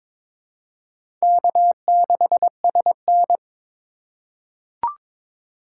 Morse code, heard as a keyed tone, about 22 wpm, 700 Hz. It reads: K6SN
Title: USA Call Signs - 4 Characters - 4x Character Spacing - 22wpm